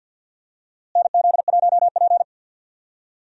Morse code, heard as a keyed tone, about 46 wpm, 700 Hz. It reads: NZ1P
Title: USA Call Signs - 4 Characters - Repeated 2x - 50wpm